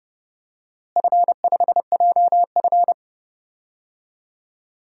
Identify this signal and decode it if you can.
Morse code, heard as a keyed tone, about 32 wpm, 700 Hz. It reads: F5JF
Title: DXCC-100 Call Signs - 4 Characters - Repeated 3x - 30wpm